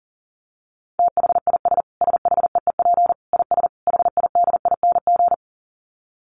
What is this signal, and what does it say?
Morse code, heard as a keyed tone, about 40 wpm, 700 Hz. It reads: THIS SHEEP IS HIDING
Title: Dit Challenge 40wpm